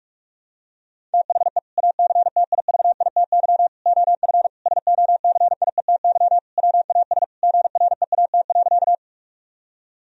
Morse code, heard as a keyed tone, about 46 wpm, 700 Hz. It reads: THE ACTIVITY OF SOCIETY WAS GREAT.
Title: Sentences from Top 1000 Words - Review Part 4 - 45wpm